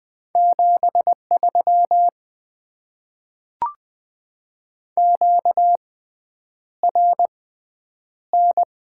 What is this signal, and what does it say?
Morse code, heard as a keyed tone, about 20 wpm, spaced as for 9 wpm, 700 Hz. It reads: 73 QRN
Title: Top 90 QSO Elements - 6x Character Spacing - 20wpm